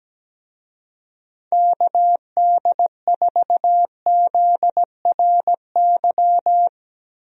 Morse code, heard as a keyed tone, about 17 wpm, 700 Hz. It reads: KD4ZRY
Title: USA Call Signs - 6 Characters - 17wpm